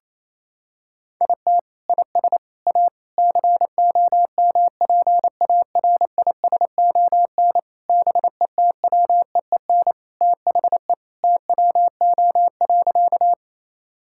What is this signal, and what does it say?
Morse code, heard as a keyed tone, about 28 wpm, 700 Hz. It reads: IT IS A COMPARISON BETWEEN THE TWO.
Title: Sentences from Top 1600 Words 28wpm